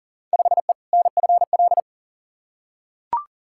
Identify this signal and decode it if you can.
Morse code, heard as a keyed tone, about 40 wpm, 700 Hz. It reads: 5E NFL
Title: Field Day Exchanges Encore - 40wpm